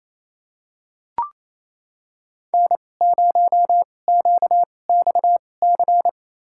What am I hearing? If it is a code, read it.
Morse code, heard as a keyed tone, about 28 wpm, spaced as for 20 wpm, 700 Hz. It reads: N0QXC
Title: USA Call Signs - 5 Characters - 2x Character Spacing - 28wpm